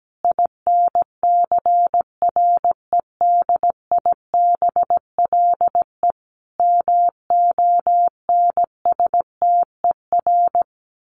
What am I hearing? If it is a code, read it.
Morse code, heard as a keyed tone, about 17 wpm, 700 Hz. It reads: INCREDIBLE MONSTER
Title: A Sound Of Thunder 17wpm